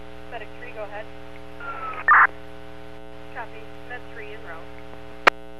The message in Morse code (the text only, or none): none